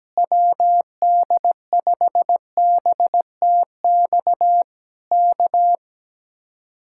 Morse code, heard as a keyed tone, about 17 wpm, 700 Hz. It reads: WD5BTX K
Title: Speed-Racing Common QSO Phrases Encore - 17wpm